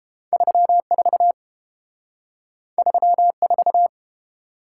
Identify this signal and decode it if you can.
Morse code, heard as a keyed tone, about 29 wpm, 700 Hz. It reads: 34 34